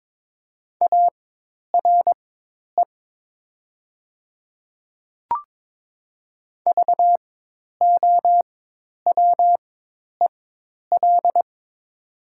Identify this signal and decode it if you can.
Morse code, heard as a keyed tone, about 22 wpm, spaced as for 10 wpm, 700 Hz. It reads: ARE VOWEL
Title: 4x Character Spacing - Top 500 Words - 22wpm